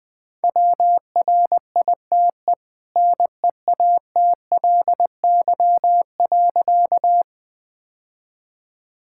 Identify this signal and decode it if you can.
Morse code, heard as a keyed tone, about 20 wpm, 700 Hz. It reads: WRITE NEATLY.
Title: Easy Sentences with Hints - 20wpm